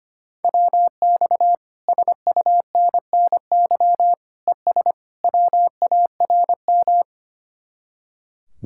Morse code, heard as a keyed tone, about 30 wpm, 700 Hz. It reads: WX SUNNY ES WARM
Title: Speed-Racing Common QSO Phrases Encore - 25wpm